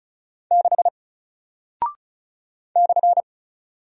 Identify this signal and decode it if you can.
Morse code, heard as a keyed tone, about 36 wpm, 700 Hz. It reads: 6 /
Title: Single Character (Letter:Number:Punctuation:Prosign) Alt 35wpm